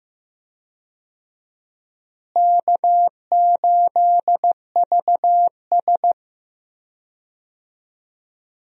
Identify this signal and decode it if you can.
Morse code, heard as a keyed tone, about 15 wpm, 700 Hz. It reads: K8VS
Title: USA Call Signs - 4 Characters - Repeated 3x - 15wpm